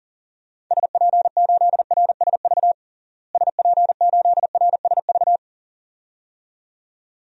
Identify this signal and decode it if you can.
Morse code, heard as a keyed tone, about 40 wpm, 700 Hz. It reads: SP8RSV SP8RSV
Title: DXCC-100 Call Signs - 6 Characters - Repeated 2x - 40wpm